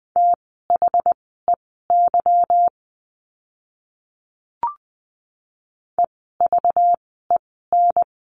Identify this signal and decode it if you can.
Morse code, heard as a keyed tone, about 20 wpm, spaced as for 14 wpm, 700 Hz. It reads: THEY EVEN
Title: 244 Lesson 39f - Letter z - Words 20wpm